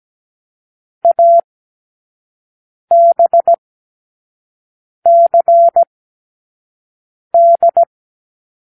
Morse code, heard as a keyed tone, about 17 wpm, spaced as for 5 wpm, 680 Hz. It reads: ABCD
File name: k7qo / Lesson6(A-D)